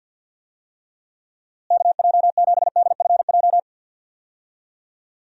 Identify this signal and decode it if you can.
Morse code, heard as a keyed tone, about 49 wpm, 700 Hz. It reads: KJ7DFJ
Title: USA Call Signs - 6 Characters - Repeated 2x - 50wpm